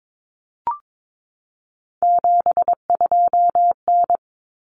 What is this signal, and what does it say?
Morse code, heard as a keyed tone, about 22 wpm, 700 Hz. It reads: 72N
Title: Common Cut Numbers - 3-Digit 22wpm